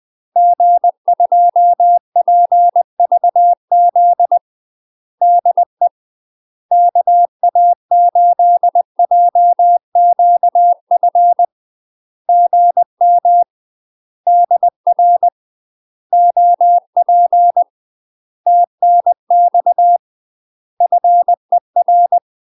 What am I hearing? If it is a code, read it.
Morse code, heard as a keyed tone, about 20 wpm, 700 Hz. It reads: G2PVZ DE KA8JQF GM DR OP TNX FER
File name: UK MASTER QSOs TRAINING / 20WPM / UK MASTER QSOs 0037